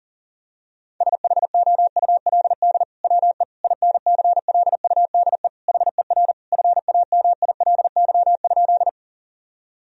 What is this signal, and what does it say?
Morse code, heard as a keyed tone, about 40 wpm, 700 Hz. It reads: SHOULD WE INCLUDE HER FAMILY?
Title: Sentences from Top 1000 Words - Review Part 3 - 40wpm